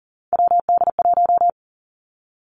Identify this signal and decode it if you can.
Morse code, heard as a keyed tone, about 42 wpm, 700 Hz. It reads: WD1
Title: Three Challenging Letter-Number Combinations 40wpm